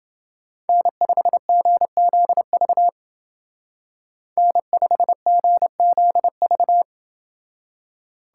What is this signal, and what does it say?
Morse code, heard as a keyed tone, about 28 wpm, 700 Hz. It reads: N5GZV N5GZV